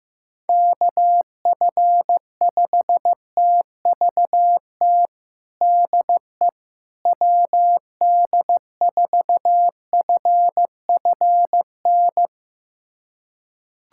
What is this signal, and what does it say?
Morse code, heard as a keyed tone, about 15 wpm, 700 Hz. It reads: KF5TVT DE WD4FFN
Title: USA Callsign DE Callsign - 6 Characters - 15wpm